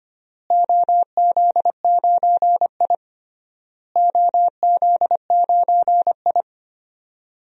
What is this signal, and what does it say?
Morse code, heard as a keyed tone, about 25 wpm, 700 Hz. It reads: OZ9I OZ9I
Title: DXCC-100 Call Signs - 4 Characters - Repeated 3x - 25wpm